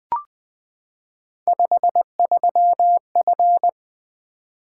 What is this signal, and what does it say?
Morse code, heard as a keyed tone, about 20 wpm, 700 Hz. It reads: 53F
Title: Three Letter-Number Combinations - 20wpm